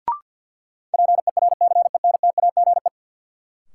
Morse code, heard as a keyed tone, about 52 wpm, 700 Hz. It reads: PERCENTAGE
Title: Sentences from Top 1300 Words - New Word Drill Alt - 50wpm